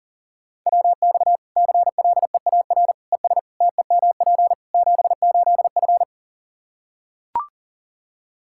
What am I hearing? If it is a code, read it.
Morse code, heard as a keyed tone, about 40 wpm, 700 Hz. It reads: WX CLEAR ES TEMP 78F
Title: Practice QSO - Exchange 2 - Info - 40wpm